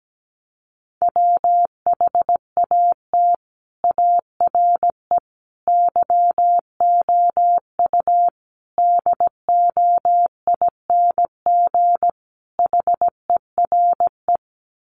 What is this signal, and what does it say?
Morse code, heard as a keyed tone, about 17 wpm, 700 Hz. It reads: WHAT ARE YOU DOING HERE
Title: Sets of 5 Words 17wpm